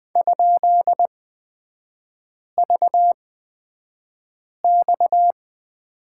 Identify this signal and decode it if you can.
Morse code, heard as a keyed tone, about 20 wpm, spaced as for 5 wpm, 690 Hz. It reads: ?VX
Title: 229 Lesson 37e - Letter x - Three Characters 20wpm